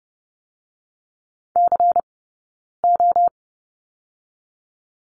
Morse code, heard as a keyed tone, about 31 wpm, 700 Hz. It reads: C O
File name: Lesson 16d - Letter c - Two Characters 30wpm